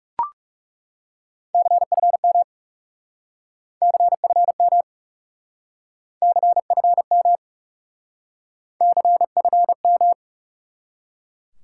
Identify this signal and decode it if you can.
Morse code, heard as a keyed tone, about 38 wpm, 700 Hz. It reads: CFM CFM CFM CFM